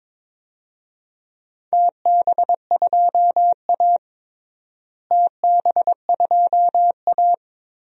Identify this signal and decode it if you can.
Morse code, heard as a keyed tone, about 22 wpm, 700 Hz. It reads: TB2A TB2A
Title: DXCC-100 Call Signs - 4 Characters - Repeated 3x - 22wpm